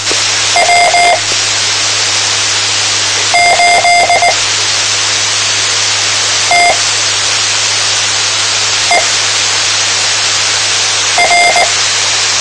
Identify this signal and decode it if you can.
Morse code, heard as a keyed tone, about 19 wpm, spaced as for 4 wpm, 700 Hz, with radio noise. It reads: W8TTR
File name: tmranets / Elmer 04 19 2015